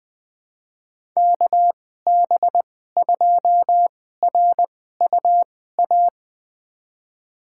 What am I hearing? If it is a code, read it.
Morse code, heard as a keyed tone, about 20 wpm, spaced as for 14 wpm, 700 Hz. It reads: KB2RUA
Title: USA Call Signs - 2x Character Spacing - 20wpm